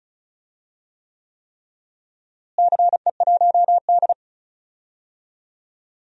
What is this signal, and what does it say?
Morse code, heard as a keyed tone, about 36 wpm, 700 Hz. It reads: CE1D
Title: DXCC-100 Call Signs - 4 Characters - Repeated 3x - 35wpm